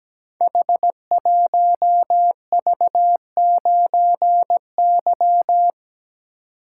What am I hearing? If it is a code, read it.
Morse code, heard as a keyed tone, about 17 wpm, 700 Hz. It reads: H1V9Y
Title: Code Groups - 5 Letter:Numbers - 17wpm